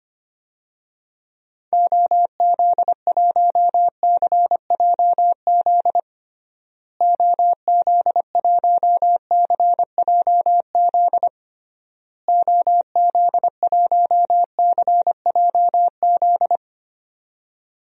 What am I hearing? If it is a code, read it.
Morse code, heard as a keyed tone, about 25 wpm, 700 Hz. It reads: OZ1CJZ OZ1CJZ OZ1CJZ
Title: DXCC-100 Call Signs - 6 Characters - Repeated 3x - 25wpm